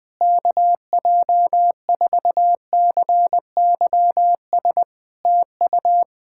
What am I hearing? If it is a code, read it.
Morse code, heard as a keyed tone, about 21 wpm, 700 Hz. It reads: KJ4CYS TU
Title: Practice QSO - Exchange 4 - Ending - 20wpm